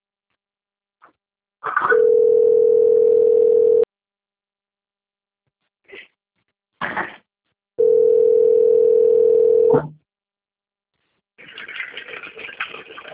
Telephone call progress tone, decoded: ringback tone

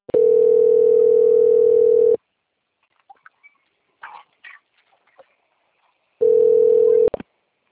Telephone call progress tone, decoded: ringback tone

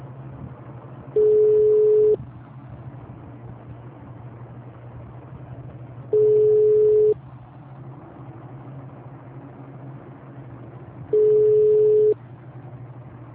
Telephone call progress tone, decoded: ringback tone

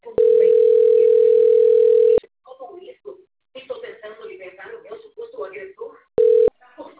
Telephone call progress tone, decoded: ringback tone

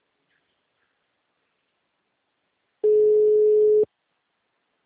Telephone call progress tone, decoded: ringback tone

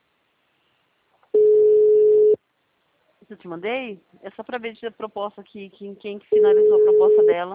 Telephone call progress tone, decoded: ringback tone

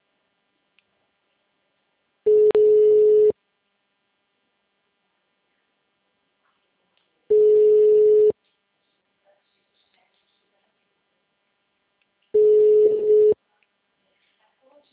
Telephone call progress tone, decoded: ringback tone

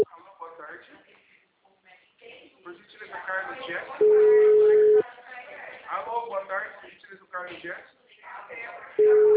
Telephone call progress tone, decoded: ringback tone

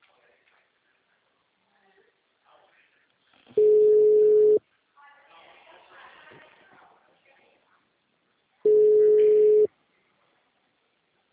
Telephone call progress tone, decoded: ringback tone

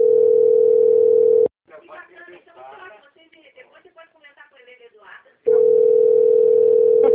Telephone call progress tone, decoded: ringback tone

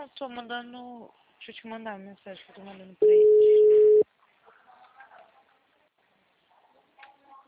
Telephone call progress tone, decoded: ringback tone

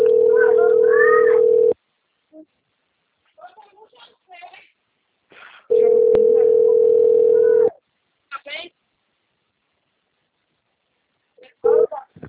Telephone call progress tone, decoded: ringback tone

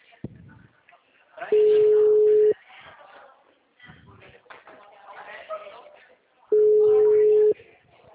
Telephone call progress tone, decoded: ringback tone

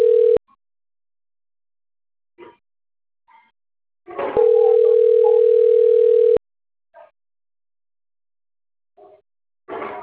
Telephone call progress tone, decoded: ringback tone

